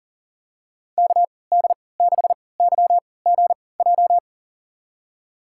Morse code, heard as a keyed tone, about 40 wpm, spaced as for 23 wpm, 700 Hz. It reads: KD6YGJ
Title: USA Call Signs - 3x Character Spacing - 40wpm